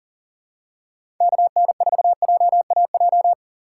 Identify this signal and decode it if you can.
Morse code, heard as a keyed tone, about 40 wpm, 700 Hz. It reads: KN4JAJ
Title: USA Call Signs - 6 Characters - 3x Character Spacing - 40wpm